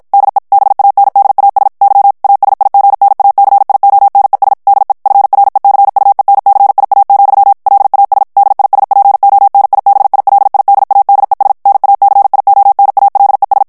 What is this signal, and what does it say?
Morse code, heard as a keyed tone, about 53 wpm, 800 Hz. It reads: DE BANDAS Y ASIGNACIONES DE FRECUENCIA, LAS DISPONIBILIDADES NACIONALES
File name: morse mp3 / ESPAGNOL TEXTE EN CLAIR 2 VITESSE 50 MOTS MINUTES